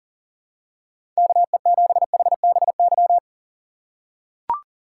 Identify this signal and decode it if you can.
Morse code, heard as a keyed tone, about 40 wpm, 700 Hz. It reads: KE7HBY
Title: USA Call Signs - 6 Characters - 2x Character Spacing - 40wpm